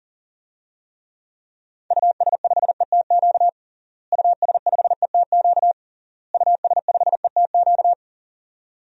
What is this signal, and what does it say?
Morse code, heard as a keyed tone, about 37 wpm, 700 Hz. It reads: US5ETQ US5ETQ US5ETQ
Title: DXCC-100 Call Signs - 6 Characters - Repeated 3x - 40wpm